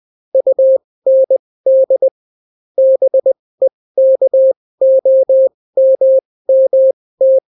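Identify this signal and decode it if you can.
Morse code, heard as a keyed tone, about 20 wpm, spaced as for 16 wpm, 530 Hz. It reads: UND BEKOMMT
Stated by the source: Daily Morse weather and top stories at 16 wpm, clean without QRN.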